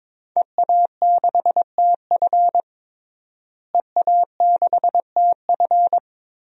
Morse code, heard as a keyed tone, about 22 wpm, 700 Hz. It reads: EA6TF EA6TF
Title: DXCC-100 Call Signs - 5 Characters - Repeated 3x - 22wpm